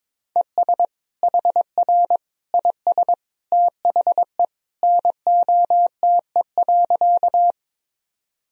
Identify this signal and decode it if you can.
Morse code, heard as a keyed tone, about 22 wpm, 700 Hz. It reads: ES HR IS THE NOTE.
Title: Sentences from Top 500 Words and Abbreviations 22wpm